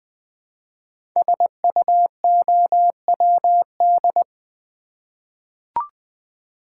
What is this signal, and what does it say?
Morse code, heard as a keyed tone, about 20 wpm, 700 Hz. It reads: SUOWD
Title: Code Groups - 5 Letters - 20wpm